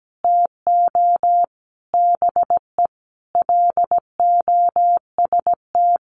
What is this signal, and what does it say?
Morse code, heard as a keyed tone, about 17 wpm, 700 Hz. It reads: TO BE LOST
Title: Sentences from Top 1400 Words 17wpm